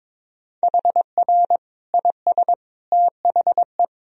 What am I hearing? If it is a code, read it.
Morse code, heard as a keyed tone, about 22 wpm, 700 Hz. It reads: HR IS THE
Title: Sentences from Top 300 Words and Abbreviations 22wpm